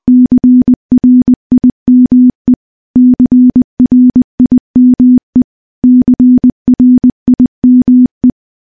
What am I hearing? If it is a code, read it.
Morse code, heard as a keyed tone, about 20 wpm, 250 Hz. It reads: CRIME CRIME CRIME
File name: morse-training-20WPM